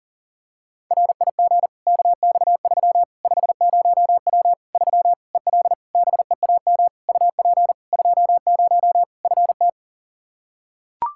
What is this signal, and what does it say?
Morse code, heard as a keyed tone, about 37 wpm, 700 Hz. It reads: RIG KX3 50W 3 EL BEAM UP 20 FT
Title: Practice QSO - Exchange 2 - Info - 40wpm